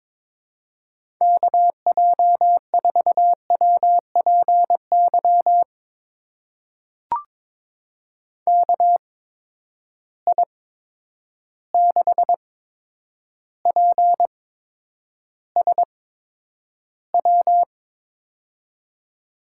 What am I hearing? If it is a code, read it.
Morse code, heard as a keyed tone, about 22 wpm, spaced as for 9 wpm, 700 Hz. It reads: KJ4WPY KI6PSW
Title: USA Call Signs - 6 Characters - 8x Character Spacing - 22wpm